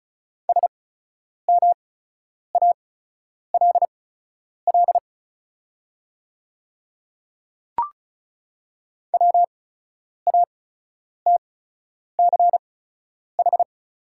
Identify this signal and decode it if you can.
Morse code, heard as a keyed tone, about 36 wpm, spaced as for 10 wpm, 700 Hz. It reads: SMALL WATCH
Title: 8x Character Spacing - Top 500 Words - 35wpm